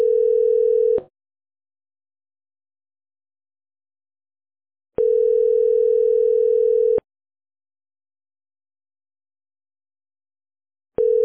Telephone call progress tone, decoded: ringback tone